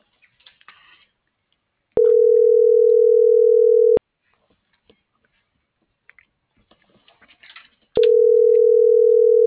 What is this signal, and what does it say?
Ringback tone (North America), 440 + 480 Hz.